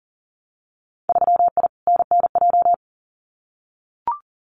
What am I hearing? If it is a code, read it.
Morse code, heard as a keyed tone, about 38 wpm, 700 Hz. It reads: 3I NNJ
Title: Winter Field Day Exchanges 40wpm